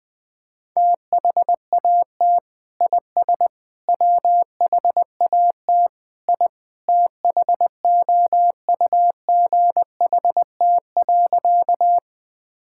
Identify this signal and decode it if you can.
Morse code, heard as a keyed tone, about 20 wpm, 700 Hz. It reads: THAT IS WHAT I THOUGHT.
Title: Sentences with 2-Word Phrases - Spoken Beginning - 20wpm